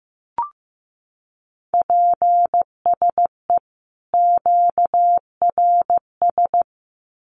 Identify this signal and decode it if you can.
Morse code, heard as a keyed tone, about 15 wpm, 700 Hz. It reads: PSE QRS
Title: Common QSO Phrases 15wpm